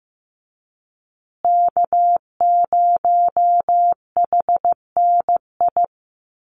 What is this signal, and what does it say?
Morse code, heard as a keyed tone, about 15 wpm, 700 Hz. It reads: K0HNI
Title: USA Call Signs Repeated 3x 15wpm